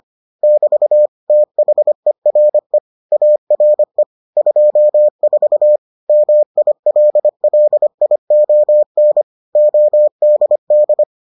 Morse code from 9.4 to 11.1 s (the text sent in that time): ODD